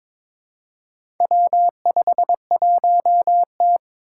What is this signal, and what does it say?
Morse code, heard as a keyed tone, about 22 wpm, 700 Hz. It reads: W51T
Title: Four Challenging Letter-Number Combinations 22wpm